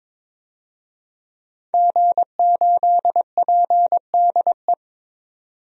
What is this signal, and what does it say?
Morse code, heard as a keyed tone, about 22 wpm, 700 Hz. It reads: G8PDE
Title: DXCC-100 Call Signs - 5 Characters - 22wpm